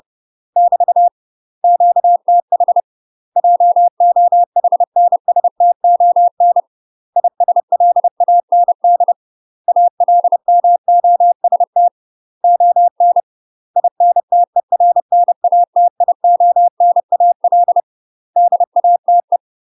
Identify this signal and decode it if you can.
Morse code, heard as a keyed tone, about 30 wpm, 700 Hz. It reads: <BT> QTH JOHNSTON ISLAND ALMOST ON INTERNATIONAL DATE